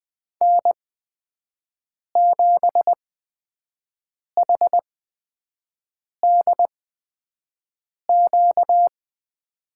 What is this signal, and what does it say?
Morse code, heard as a keyed tone, about 20 wpm, spaced as for 5 wpm, 700 Hz. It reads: N7HDQ
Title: USA Call Signs - 5 Characters - 8x Character Spacing - 20wpm